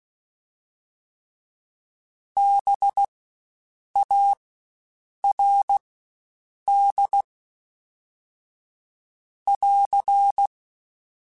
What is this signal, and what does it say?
Morse code, heard as a keyed tone, about 16 wpm, spaced as for 7 wpm, 800 Hz. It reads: BARD <AR>
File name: Spaced Alphas / Spaced Alphas mp3s / Bs sp @15